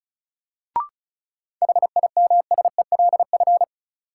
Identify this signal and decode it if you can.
Morse code, heard as a keyed tone, about 34 wpm, 700 Hz. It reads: HIMSELF
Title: Sentences from Top 1500 Words - New Word Drill - 35wpm